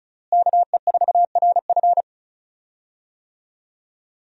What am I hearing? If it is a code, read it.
Morse code, heard as a keyed tone, about 35 wpm, 700 Hz. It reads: KE4RF
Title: USA Call Signs - 5 Characters - Repeated 3x - 35wpm